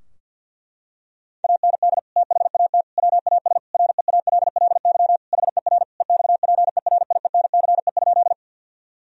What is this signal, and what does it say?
Morse code, heard as a keyed tone, about 51 wpm, 700 Hz. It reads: AND THAT WAS REALLY HER EXPERIENCE?
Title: Sentences from Top 1000 Words - Review Part 3 - 50wpm